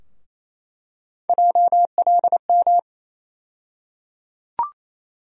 Morse code, heard as a keyed tone, about 28 wpm, 700 Hz. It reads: JLM
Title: USA Call Sign Postfixes - 3 Letters 28wpm